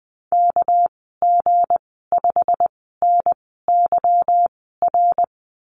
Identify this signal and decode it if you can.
Morse code, heard as a keyed tone, about 20 wpm, spaced as for 15 wpm, 700 Hz. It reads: KG5NYR
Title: 238 Lesson 38g - Letter q - Call Signs 20wpm